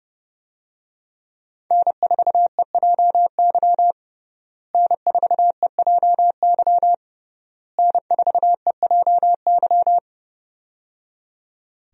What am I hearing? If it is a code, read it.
Morse code, heard as a keyed tone, about 30 wpm, 700 Hz. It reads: N4EJY N4EJY N4EJY
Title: USA Call Signs - 5 Characters - Repeated 3x - 30wpm